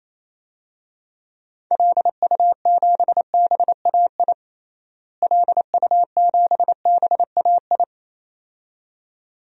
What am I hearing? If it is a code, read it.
Morse code, heard as a keyed tone, about 28 wpm, 700 Hz. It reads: LU7BAI LU7BAI
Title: DXCC-100 Call Signs - 6 Characters - Repeated 2x - 28wpm